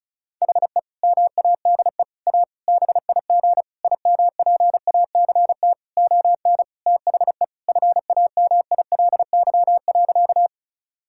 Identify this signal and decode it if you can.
Morse code, heard as a keyed tone, about 34 wpm, 700 Hz. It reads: HE MADE A BIG IMPACT ON THE FAMILY.
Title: Sentences from Top 1000 Words - Review Part 6 - 35wpm